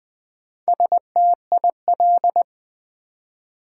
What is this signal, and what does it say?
Morse code, heard as a keyed tone, about 20 wpm, 700 Hz. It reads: STIL
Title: Build A Word - 1 Letter At A Time - 5-Letter Words - 20wpm